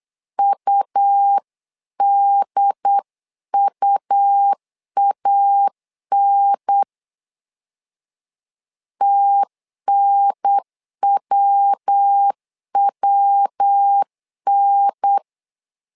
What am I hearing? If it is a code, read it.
Morse code, heard as a keyed tone, about 8 wpm, 800 Hz. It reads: UDUAN TNWWN
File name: LEZIONI / 3 MWRO / 3 MW 8wpm 2501